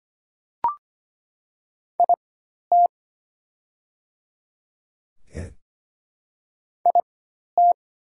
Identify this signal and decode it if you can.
Morse code, heard as a keyed tone, about 25 wpm, spaced as for 12 wpm, 690 Hz. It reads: IT IT
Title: 4x Character Spacing - Top 100 Words - 25wpm